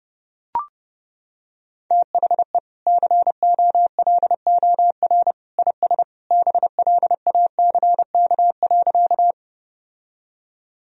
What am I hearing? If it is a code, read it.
Morse code, heard as a keyed tone, about 30 wpm, 700 Hz. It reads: THE COLOR IS BLACK.
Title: Sentences from Top 1000 Words - Review Part 2 - 30wpm